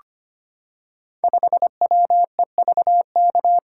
Morse code, heard as a keyed tone, about 25 wpm, 700 Hz. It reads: 5WEVK
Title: Code Groups - 5 Letter:Numbers - 25wpm